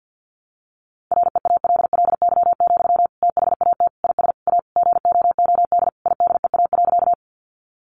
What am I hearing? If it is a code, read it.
Morse code, heard as a keyed tone, about 50 wpm, 700 Hz. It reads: REALLY, THAT IS A GOOD IDEA.
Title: Sentences from Top 600 Words 50wpm